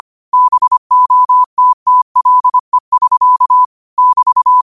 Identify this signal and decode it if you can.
Morse code, heard as a keyed tone, about 25 wpm, 1000 Hz. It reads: BOTTLE <SK> <BT>
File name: qotdfiles / 25wpm